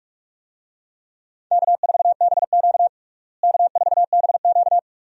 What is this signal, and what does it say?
Morse code, heard as a keyed tone, about 46 wpm, 700 Hz. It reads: K4BQ K4BQ
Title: USA Call Signs - 4 Characters - Repeated 2x - 45wpm